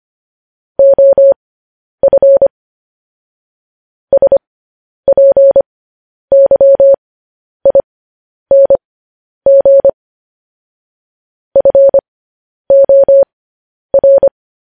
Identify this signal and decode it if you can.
Morse code, heard as a keyed tone, about 25 wpm, spaced as for 10 wpm, 550 Hz. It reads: OF SPYING FOR
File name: podcast / News headlines-1x-25x10